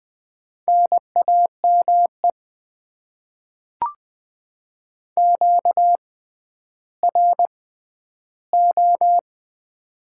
Morse code, heard as a keyed tone, about 20 wpm, spaced as for 11 wpm, 700 Hz. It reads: NAME QRO
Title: Top 90 QSO Elements - 6x Character Spacing - 20wpm